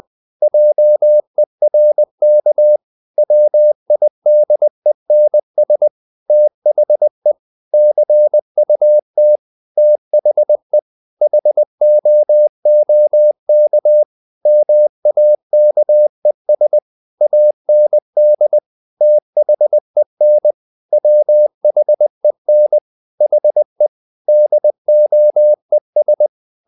Morse code, heard as a keyed tone, about 20 wpm, 600 Hz. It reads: JERK WIDENS THE CUT THE HOOK MAKES AND THEN WHEN HE DOES
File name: the old man and the sea / the old man and the sea part 1 20wpm